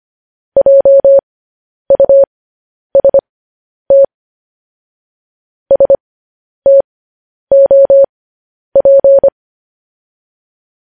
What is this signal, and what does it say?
Morse code, heard as a keyed tone, about 25 wpm, spaced as for 10 wpm, 550 Hz. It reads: JUST STOP
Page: news-headlines-in-morse-code-at-10-wpm